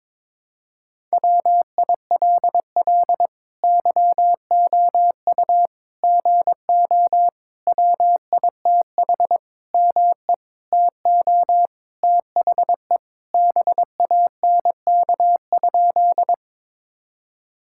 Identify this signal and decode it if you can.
Morse code, heard as a keyed tone, about 22 wpm, 700 Hz. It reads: WILL YOU GO WITH ME TO THE BANK?
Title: Sentences from Top 1000 Words - Review Part 6 - 22wpm